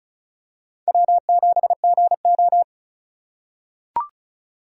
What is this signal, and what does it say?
Morse code, heard as a keyed tone, about 36 wpm, 700 Hz. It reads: W7GO
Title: USA Call Signs - 4 Characters - 2x Character Spacing - 35wpm